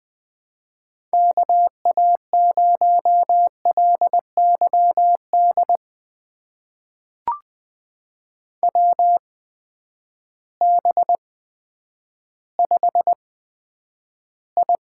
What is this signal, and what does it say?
Morse code, heard as a keyed tone, about 20 wpm, spaced as for 10 wpm, 700 Hz. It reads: KA0LYD WB5I
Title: USA Call Signs - 6 Characters - 8x Character Spacing - 20wpm